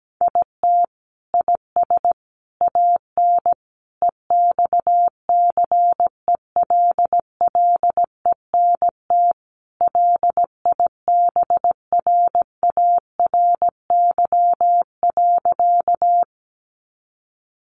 Morse code, heard as a keyed tone, about 17 wpm, 700 Hz. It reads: IT IS AN EXCELLENT LIBRARY.
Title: Sentences from Top 1000 Words 17wpm